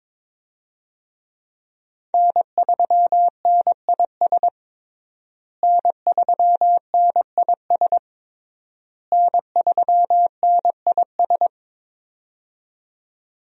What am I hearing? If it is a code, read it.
Morse code, heard as a keyed tone, about 21 wpm, 700 Hz. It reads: N3NIS N3NIS N3NIS
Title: USA Call Signs - 5 Characters - Repeated 3x - 22wpm